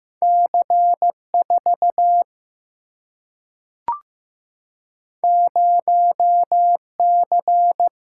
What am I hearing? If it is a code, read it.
Morse code, heard as a keyed tone, about 15 wpm, 700 Hz. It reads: C 4 0 C
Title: Single-Letter Single-Number 15wpm